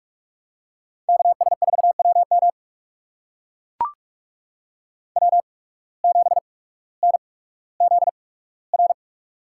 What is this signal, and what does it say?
Morse code, heard as a keyed tone, about 44 wpm, spaced as for 19 wpm, 700 Hz. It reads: KS4WM W7NZR
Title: USA Call Signs - 8x Character Spacing - 45wpm